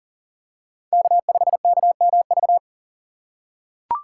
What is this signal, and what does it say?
Morse code, heard as a keyed tone, about 40 wpm, 700 Hz. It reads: K5KMV
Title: USA Call Signs - 5 Characters - 3x Character Spacing - 40wpm